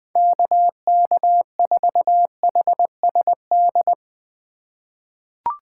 Morse code, heard as a keyed tone, about 21 wpm, 700 Hz. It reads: KK4HSD
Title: USA Call Signs - 4x Character Spacing - 20wpm